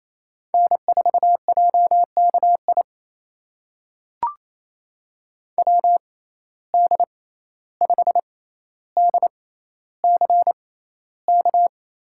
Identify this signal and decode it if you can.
Morse code, heard as a keyed tone, about 28 wpm, spaced as for 14 wpm, 700 Hz. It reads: N4JKI WD5DCK
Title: USA Call Signs - 6x Character Spacing - 28wpm